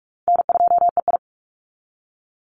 Morse code, heard as a keyed tone, about 44 wpm, 700 Hz. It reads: N2EI
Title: Call Signs Repeated 3x - 45wpm